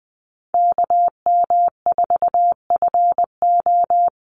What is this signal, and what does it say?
Morse code, heard as a keyed tone, about 20 wpm, 700 Hz. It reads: KM4FO
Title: Call Signs - 20wpm